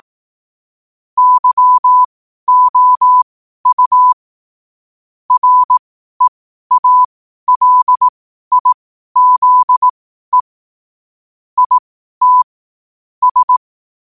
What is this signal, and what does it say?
Morse code, heard as a keyed tone, about 18 wpm, spaced as for 12 wpm, 1000 Hz. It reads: YOU REALIZE IT S
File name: qotdfiles / 12wpm